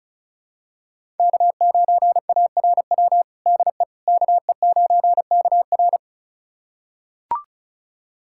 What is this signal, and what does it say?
Morse code, heard as a keyed tone, about 36 wpm, 700 Hz. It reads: K9ARW DE KE9KR
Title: USA Callsign DE Callsign - 5 Characters - 35wpm